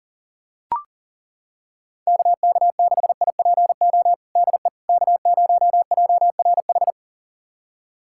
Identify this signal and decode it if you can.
Morse code, heard as a keyed tone, about 40 wpm, 700 Hz. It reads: KK6IPO DE K0JRH
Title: USA Callsign DE Callsign 40wpm